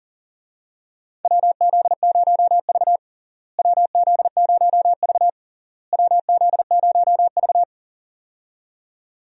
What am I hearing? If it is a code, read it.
Morse code, heard as a keyed tone, about 40 wpm, 700 Hz. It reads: WZ0V WZ0V WZ0V
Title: USA Call Signs - 4 Characters - Repeated 3x - 40wpm